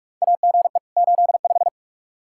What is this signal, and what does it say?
Morse code, heard as a keyed tone, about 43 wpm, 700 Hz. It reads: AGE 85
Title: Practice QSO - Exchange 3 - More Info - 45wpm